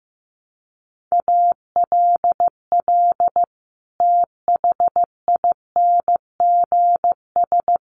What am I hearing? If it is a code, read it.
Morse code, heard as a keyed tone, about 15 wpm, 700 Hz. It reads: ALL THINGS
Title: Sets of 5 Words Encore 15wpm